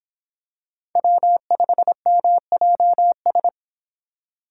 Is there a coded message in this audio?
Yes: Morse code from 1.0 to 3.5 s, heard as a keyed tone, about 25 wpm, 700 Hz. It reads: W5MJS